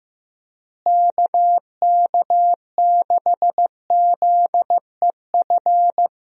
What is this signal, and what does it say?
Morse code, heard as a keyed tone, about 15 wpm, 700 Hz. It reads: KK6ZEF
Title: USA Call Signs - 6 Characters - Repeated 3x - 15wpm